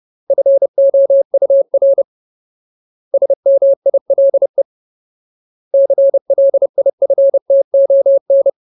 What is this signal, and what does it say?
Morse code, heard as a keyed tone, about 30 wpm, 550 Hz. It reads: FOUR SMILE CLIFTON
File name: Ultimate Word List / 30WPM Ultimate